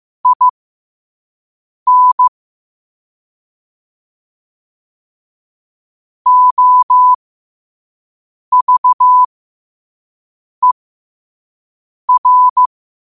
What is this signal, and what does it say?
Morse code, heard as a keyed tone, about 15 wpm, spaced as for 5 wpm, 1000 Hz. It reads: IN OVER
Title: 5 WPM morse code quotes for Fri, 01 Mar 2024